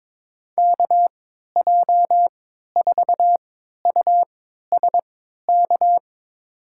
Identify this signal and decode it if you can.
Morse code, heard as a keyed tone, about 22 wpm, spaced as for 12 wpm, 700 Hz. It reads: KJ4USK
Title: USA Call Signs - 6 Characters - 3x Character Spacing - 22wpm